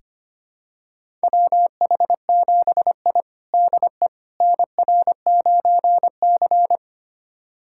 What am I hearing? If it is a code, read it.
Morse code, heard as a keyed tone, about 25 wpm, 700 Hz. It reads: WH7I DE NR9C
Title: USA Callsign DE Callsign - 4 Characters - 25wpm